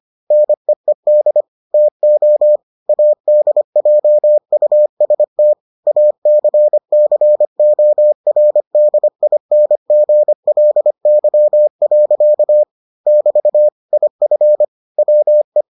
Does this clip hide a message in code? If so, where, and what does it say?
Yes: Morse code from 0.3 to 15.6 s, heard as a keyed tone, about 25 wpm, 600 Hz. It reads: NEED TO ADJUST ACCORDINGLY. <BT> IF WE